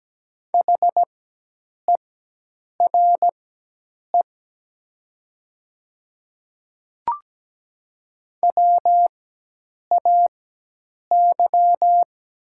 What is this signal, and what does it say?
Morse code, heard as a keyed tone, about 17 wpm, spaced as for 8 wpm, 700 Hz. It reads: HERE WAY
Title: 4x Character Spacing - Top 300 Words - 17wpm